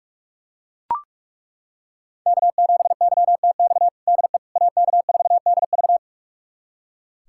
Morse code, heard as a keyed tone, about 41 wpm, 700 Hz. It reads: K7YTX DE AK4DV
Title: USA Callsign DE Callsign - 5 Characters - 45wpm